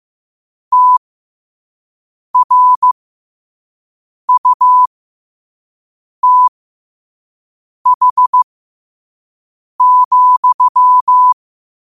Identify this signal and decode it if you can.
Morse code, heard as a keyed tone, about 15 wpm, spaced as for 5 wpm, 1000 Hz. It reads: TRUTH,
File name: qotdfiles / 5wpm